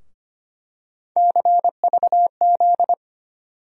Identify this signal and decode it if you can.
Morse code, heard as a keyed tone, about 25 wpm, 700 Hz. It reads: CVZ
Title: USA Call Sign Postfixes - 3 Letters 25wpm